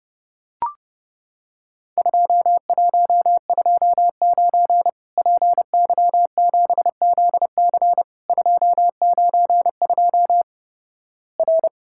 Morse code, heard as a keyed tone, about 32 wpm, 700 Hz. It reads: 2129 PY7ZC 292 R
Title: Worked All Europe DX Contest - QTC Exchange - 30wpm